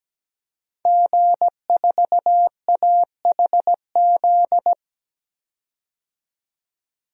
Morse code, heard as a keyed tone, about 17 wpm, 700 Hz. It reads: G4AHZ
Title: DXCC-100 Call Signs - 5 Characters - 17wpm